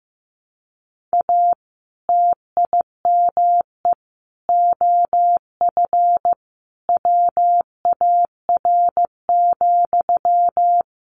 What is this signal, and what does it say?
Morse code, heard as a keyed tone, about 15 wpm, 700 Hz. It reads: A TIME OF WAR,
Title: Sentences from Top 1500 Words 15wpm